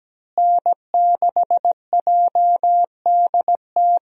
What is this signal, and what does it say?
Morse code, heard as a keyed tone, about 17 wpm, 700 Hz. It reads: N6JDT
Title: USA Call Signs - 5 Characters - Repeated 3x - 17wpm